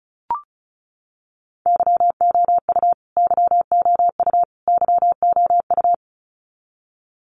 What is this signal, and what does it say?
Morse code, heard as a keyed tone, about 34 wpm, 700 Hz. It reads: YOU YOU YOU
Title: Top 200 Words Repeated 35wpm